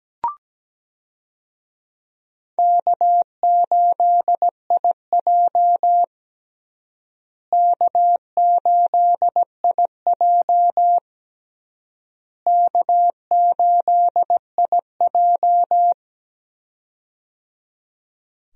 Morse code, heard as a keyed tone, about 17 wpm, 700 Hz. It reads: K8IJ K8IJ K8IJ
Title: USA Call Signs - 4 Characters - Repeated 3x - 17wpm